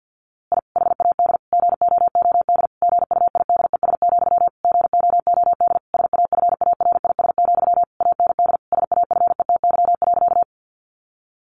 Morse code, heard as a keyed tone, about 50 wpm, 700 Hz. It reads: I HAD GOOD GUIDES, GOOD SAFARIS, AND SAFETY.
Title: A Sound Of Thunder 50wpm